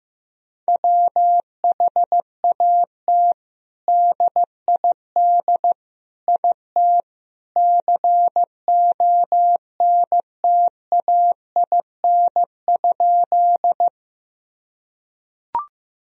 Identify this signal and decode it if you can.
Morse code, heard as a keyed tone, about 15 wpm, 700 Hz. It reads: WHAT DID IT CONTAIN?
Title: Sentences from Top 1000 Words - Review Part 2 - 15wpm